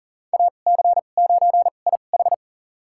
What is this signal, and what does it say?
Morse code, heard as a keyed tone, about 40 wpm, spaced as for 29 wpm, 700 Hz. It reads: AC9IH
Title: USA Call Signs - 5 Characters - 2x Character Spacing - 40wpm